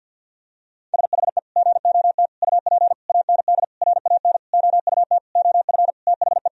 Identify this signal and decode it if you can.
Morse code, heard as a keyed tone, about 50 wpm, 700 Hz. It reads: SHE GOT UP AND RAN OUT OF THE
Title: Sentences from Top 1000 Words - Review Part 3 - 50wpm